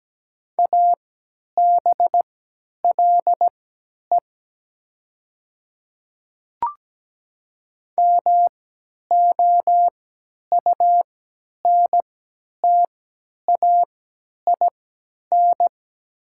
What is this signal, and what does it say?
Morse code, heard as a keyed tone, about 17 wpm, spaced as for 10 wpm, 690 Hz. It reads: ABLE MOUNTAIN
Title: 3x Character Spacing - Top 500 Words - 17wpm